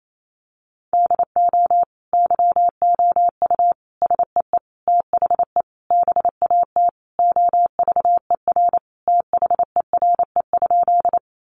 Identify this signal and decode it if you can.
Morse code, heard as a keyed tone, about 28 wpm, 700 Hz. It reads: DO YOU SEE THE BAT OVER THERE?
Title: Sentences from Top 1500 Words 28wpm